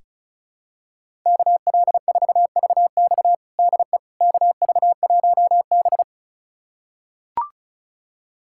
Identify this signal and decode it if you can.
Morse code, heard as a keyed tone, about 34 wpm, 700 Hz. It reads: KL4VX DE KV1B
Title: USA Callsign DE Callsign 35wpm